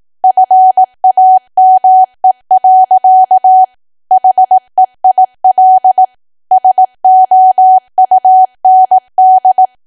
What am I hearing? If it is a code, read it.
Morse code, heard as a keyed tone, about 18 wpm, 750 Hz. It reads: FAME. HEIL SOUND
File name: Archive / 18 WPM / 18WPM